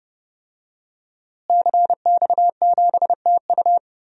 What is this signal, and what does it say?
Morse code, heard as a keyed tone, about 30 wpm, 700 Hz. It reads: CX7TU
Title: DXCC-100 Call Signs - 5 Characters - Repeated 3x - 30wpm